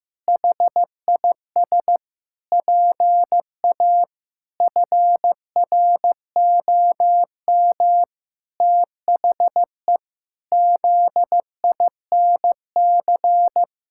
Morse code, heard as a keyed tone, about 15 wpm, 700 Hz. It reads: HIS PA FROM THE ZINC
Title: Pangram Challenge v2 - 15wpm